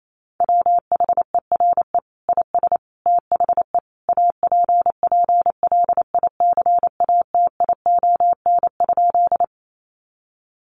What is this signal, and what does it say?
Morse code, heard as a keyed tone, about 28 wpm, 700 Hz. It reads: WHERE IS THE APPLICATION?
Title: Sentences from Top 1200 Words 28wpm